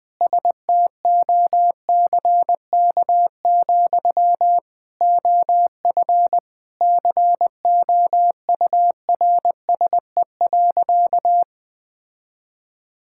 Morse code, heard as a keyed tone, about 20 wpm, 700 Hz. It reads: STOCK, OF COURSE.
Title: Sentences from Top 1000 Words - Review Part 5 - 20wpm